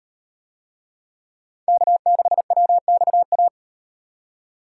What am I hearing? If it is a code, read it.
Morse code, heard as a keyed tone, about 34 wpm, 700 Hz. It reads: K6WXA